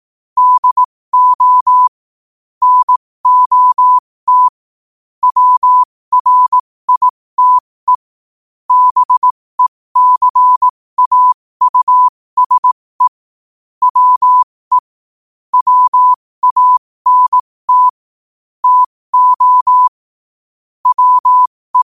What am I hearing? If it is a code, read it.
Morse code, heard as a keyed tone, about 18 wpm, spaced as for 15 wpm, 1000 Hz. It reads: DO NOT WRITE BECAUSE WE WANT TO WE
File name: qotdfiles / 15wpm